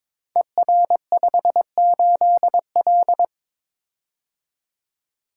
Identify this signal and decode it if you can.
Morse code, heard as a keyed tone, about 22 wpm, 700 Hz. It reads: ER58L
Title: Code Groups - 5 Letter:Numbers - 22wpm